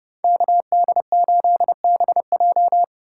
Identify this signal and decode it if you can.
Morse code, heard as a keyed tone, about 29 wpm, 700 Hz. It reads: KD8BJ